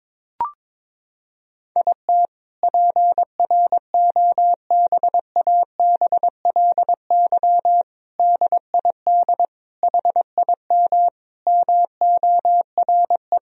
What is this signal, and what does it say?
Morse code, heard as a keyed tone, about 22 wpm, 700 Hz. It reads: IT PROBABLY DID HIM MORE
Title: Sentences from Top 1000 Words - Review Part 4 - 22wpm